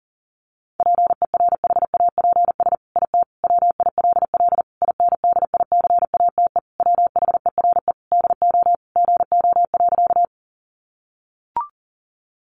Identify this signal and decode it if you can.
Morse code, heard as a keyed tone, about 40 wpm, 700 Hz. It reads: PERHAPS IT WILL INDICATE WHERE DO GO.
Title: Sentences from Top 1300 Words 40wpm